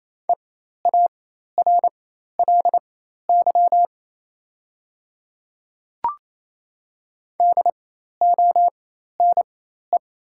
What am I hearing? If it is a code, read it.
Morse code, heard as a keyed tone, about 28 wpm, spaced as for 13 wpm, 700 Hz. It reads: EARLY DONE
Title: 4x Character Spacing - Top 500 Words - 28wpm